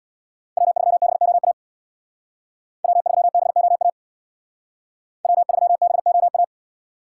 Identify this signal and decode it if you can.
Morse code, heard as a keyed tone, about 62 wpm, 700 Hz. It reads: W3BOA W3BOA W3BOA
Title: Speed Racing - USA 1x3 Call Signs 50wpm